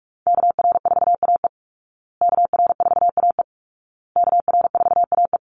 Morse code, heard as a keyed tone, about 44 wpm, 700 Hz. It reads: KR4AE KR4AE KR4AE
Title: Call Signs Repeated 2x - 45wpm